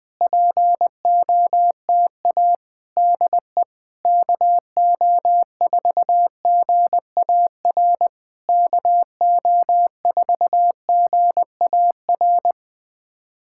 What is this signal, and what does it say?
Morse code, heard as a keyed tone, about 20 wpm, 700 Hz. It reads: POTA DE KO4GAR KO4GAR
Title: POTA QSO Practice - 20wpm